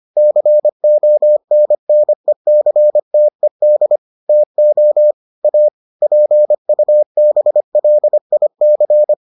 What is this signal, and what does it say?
Morse code, heard as a keyed tone, about 25 wpm, 600 Hz. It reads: CONNECTED TO A PUBLIC